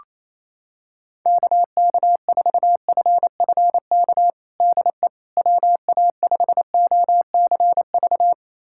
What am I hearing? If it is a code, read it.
Morse code, heard as a keyed tone, about 28 wpm, 700 Hz. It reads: KK4FFK DE WA5OCV
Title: USA Callsign DE Callsign - 6 Characters - 28wpm